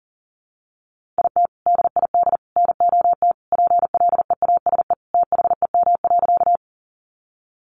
Morse code, heard as a keyed tone, about 40 wpm, 700 Hz. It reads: IT DID NOT PLEASE THEM.
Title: Sentences from Top 1500 Words 40wpm